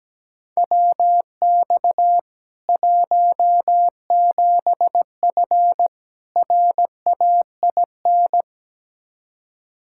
Morse code, heard as a keyed tone, about 17 wpm, 700 Hz. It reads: WX 17F RAIN
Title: K1USN SST QSO Practice 17wpm